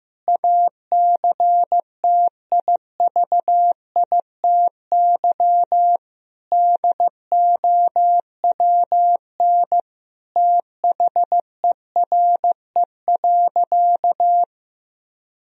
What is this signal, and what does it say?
Morse code, heard as a keyed tone, about 16 wpm, 700 Hz. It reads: ACTIVITY DOWN THERE.
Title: Sentences from Top 1000 Words - Review Part 4 - 15wpm